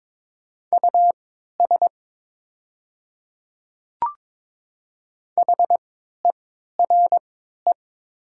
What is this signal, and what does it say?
Morse code, heard as a keyed tone, about 22 wpm, spaced as for 13 wpm, 700 Hz. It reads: US HERE
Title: 3x Character Spacing - Top 200 Words - 22wpm